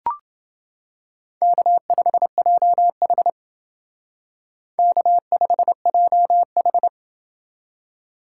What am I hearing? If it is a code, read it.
Morse code, heard as a keyed tone, about 29 wpm, 700 Hz. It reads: K5JH K5JH